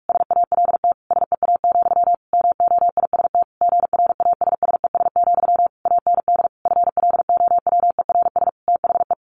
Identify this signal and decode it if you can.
Morse code, heard as a keyed tone, about 44 wpm, 700 Hz. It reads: SALT SEA, MOIST GRASSES, AND FLOWERS THE
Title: A Sound Of Thunder 45wpm